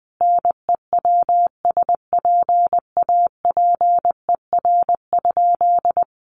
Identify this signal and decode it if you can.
Morse code, heard as a keyed tone, about 19 wpm, 700 Hz. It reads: NEWSPAPER?
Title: Sentences from Top 1100 Words 20wpm